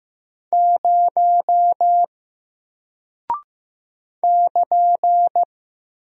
Morse code, heard as a keyed tone, about 15 wpm, 700 Hz. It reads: 0 <KN>
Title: Single Character (Letter:Number:Punctuation:Prosign) Alt 15wpm